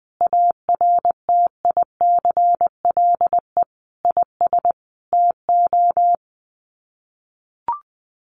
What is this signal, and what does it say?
Morse code, heard as a keyed tone, about 20 wpm, 700 Hz. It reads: ARTICLE IS TO
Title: Sets of 5 Words Encore2 20wpm